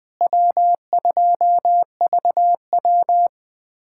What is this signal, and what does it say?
Morse code, heard as a keyed tone, about 20 wpm, 700 Hz. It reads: W2VW
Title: USA Callsign DE Callsign - 4 Characters - 20wpm